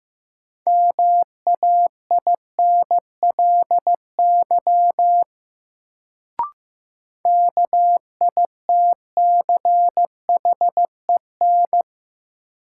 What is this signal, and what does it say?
Morse code, heard as a keyed tone, about 15 wpm, 700 Hz. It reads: MAINLY KITCHEN
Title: Sentences from Top 1400 Words - New Word Drill - 15wpm